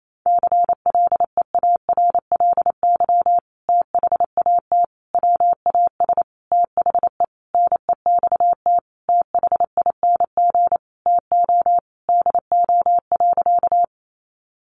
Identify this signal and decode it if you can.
Morse code, heard as a keyed tone, about 28 wpm, 700 Hz. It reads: CLEARLY THAT WAS THE NEXT THING TO DO.
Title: Sentences from Top 1300 Words 28wpm